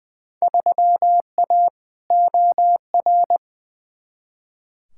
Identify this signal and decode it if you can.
Morse code, heard as a keyed tone, about 20 wpm, 700 Hz. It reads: 3A OR
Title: Field Day Exchanges Encore - 20wpm